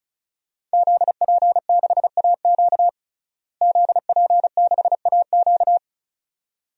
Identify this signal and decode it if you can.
Morse code, heard as a keyed tone, about 34 wpm, 700 Hz. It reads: ZP6AQ ZP6AQ
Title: DXCC-100 Call Signs - 5 Characters - Repeated 2x - 35wpm